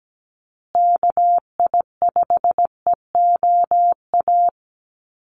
Morse code, heard as a keyed tone, about 17 wpm, 700 Hz. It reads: KI5EOA
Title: USA Call Signs Repeated 3x 17wpm